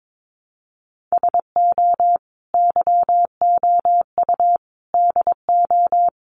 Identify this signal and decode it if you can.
Morse code, heard as a keyed tone, about 22 wpm, 700 Hz. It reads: SO YOU DO
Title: Sets of 5 Words Encore 22wpm